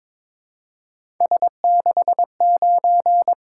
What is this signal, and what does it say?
Morse code, heard as a keyed tone, about 22 wpm, 700 Hz. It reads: S69
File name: practice / 22wpm / Three Letter-Number Combinations - 22wpm